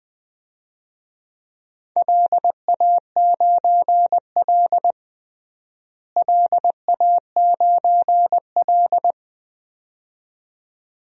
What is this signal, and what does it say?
Morse code, heard as a keyed tone, about 20 wpm, 700 Hz. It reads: LA9L LA9L
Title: DXCC-100 Call Signs - 4 Characters - Repeated 2x - 20wpm